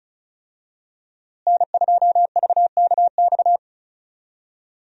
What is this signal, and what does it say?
Morse code, heard as a keyed tone, about 34 wpm, 700 Hz. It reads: N2VKX
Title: USA Call Signs - 5 Characters - Repeated 2x - 35wpm